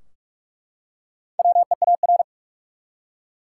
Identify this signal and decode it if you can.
Morse code, heard as a keyed tone, about 43 wpm, 700 Hz. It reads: WEAR
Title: Four Letter Words - 45wpm